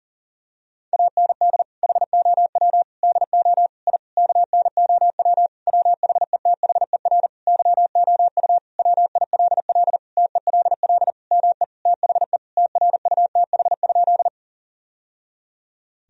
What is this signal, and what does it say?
Morse code, heard as a keyed tone, about 37 wpm, 700 Hz. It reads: AND HOW DO I KNOW WHETHER YOU WILL TELL ME THE TRUTH?
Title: Sentences from Top 1000 Words - Review Part 3 - 40wpm